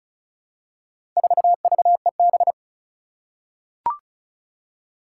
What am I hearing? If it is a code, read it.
Morse code, heard as a keyed tone, about 36 wpm, 700 Hz. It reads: 4VEB
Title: Four Challenging Letter-Number Combinations 35wpm